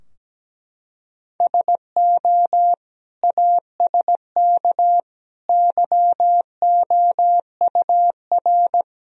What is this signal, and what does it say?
Morse code, heard as a keyed tone, about 17 wpm, 700 Hz. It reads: SO ASK YOUR
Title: Sentences from Top 1000 Words - Review Part 2 - 17wpm